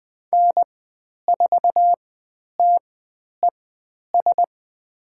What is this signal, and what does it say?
Morse code, heard as a keyed tone, about 20 wpm, spaced as for 10 wpm, 700 Hz. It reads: N4TES
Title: 102 Lesson 17g - Letter u - Call Signs 20wpm